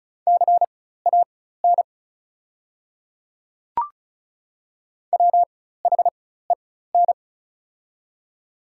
Morse code, heard as a keyed tone, about 36 wpm, spaced as for 16 wpm, 690 Hz. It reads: CAN WHEN
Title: 4x Character Spacing - Top 300 Words - 35wpm